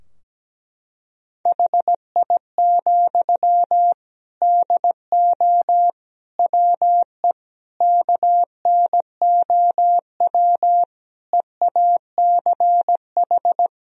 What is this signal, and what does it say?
Morse code, heard as a keyed tone, about 17 wpm, 700 Hz. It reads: HI, DO WE KNOW EACH
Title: Sentences from Top 1000 Words - Review Part 3 - 17wpm